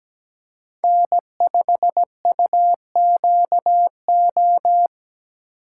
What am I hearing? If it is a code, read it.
Morse code, heard as a keyed tone, about 17 wpm, 700 Hz. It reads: N5UQO
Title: USA Call Signs - 4x Character Spacing - 17wpm